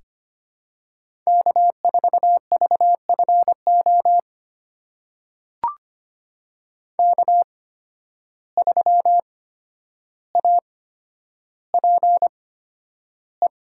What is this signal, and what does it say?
Morse code, heard as a keyed tone, about 25 wpm, spaced as for 11 wpm, 700 Hz. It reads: K4VFO K3APE
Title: USA Call Signs - 5 Characters - 8x Character Spacing - 25wpm